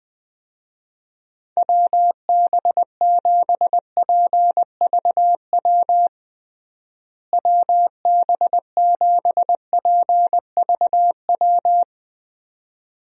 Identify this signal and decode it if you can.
Morse code, heard as a keyed tone, about 20 wpm, 700 Hz. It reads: WB7PVW WB7PVW
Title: USA Call Signs - 6 Characters - Repeated 2x - 20wpm